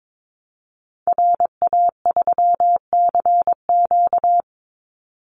Morse code, heard as a keyed tone, about 22 wpm, 700 Hz. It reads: RA3CQ
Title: Call Signs Repeated 3x - 22wpm